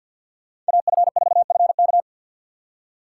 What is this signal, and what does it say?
Morse code, heard as a keyed tone, about 64 wpm, 690 Hz. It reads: AF4FK